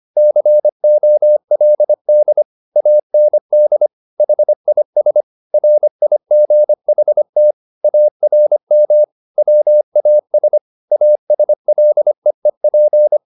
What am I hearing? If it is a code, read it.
Morse code, heard as a keyed tone, about 25 wpm, 600 Hz. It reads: COLD AND HIS RIGHT ARM WAS ASLEEP